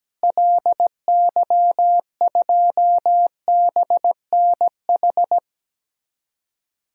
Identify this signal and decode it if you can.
Morse code, heard as a keyed tone, about 17 wpm, 700 Hz. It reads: LY2BNH
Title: DXCC-100 Call Signs - 6 Characters - Repeated 3x - 17wpm